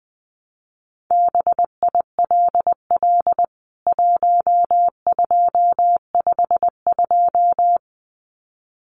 Morse code, heard as a keyed tone, about 20 wpm, 700 Hz. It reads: BILL 1252
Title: CWT Exchange 20wpm